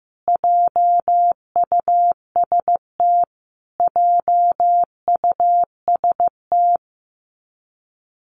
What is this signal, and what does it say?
Morse code, heard as a keyed tone, about 15 wpm, 700 Hz. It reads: JUST JUST
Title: Top 500 Words Repeated 15wpm